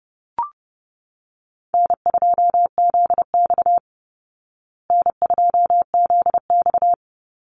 Morse code, heard as a keyed tone, about 30 wpm, 700 Hz. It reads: N2ZX N2ZX
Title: Call Signs Repeated 3x - 30wpm